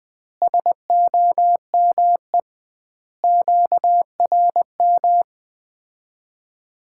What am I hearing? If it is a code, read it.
Morse code, heard as a keyed tone, about 21 wpm, 700 Hz. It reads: SOME QRM
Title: Common QSO Phrases - 2x Word-Spacing - 20wpm